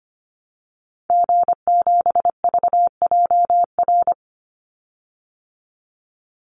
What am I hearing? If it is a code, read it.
Morse code, heard as a keyed tone, about 25 wpm, 700 Hz. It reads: G7VJR
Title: Call Signs Repeated 3x - 25wpm